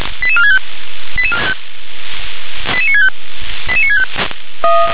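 Five-tone selective call (ZVEI), heard as quick runs of five tones, tones of about 70 ms. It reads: 88465 88965 88965
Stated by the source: ZVEI five-tone selective call received by SDR.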